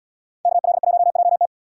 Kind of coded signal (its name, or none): Morse code